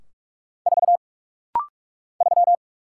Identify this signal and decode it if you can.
Morse code, heard as a keyed tone, about 46 wpm, 700 Hz. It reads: 4 3
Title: Single Letter-Number - Rapid-Fire 45wpm